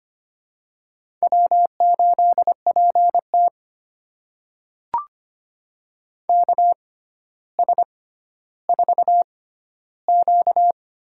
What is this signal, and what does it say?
Morse code, heard as a keyed tone, about 25 wpm, spaced as for 13 wpm, 700 Hz. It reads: W8PT KS4Q
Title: USA Call Signs - 4 Characters - 6x Character Spacing - 25wpm